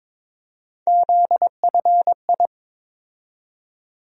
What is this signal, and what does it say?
Morse code, heard as a keyed tone, about 22 wpm, 700 Hz. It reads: ZFI
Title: USA Call Sign Postfixes - 3 Letters 22wpm